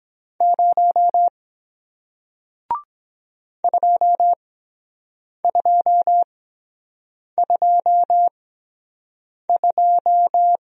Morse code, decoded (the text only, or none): 0 2222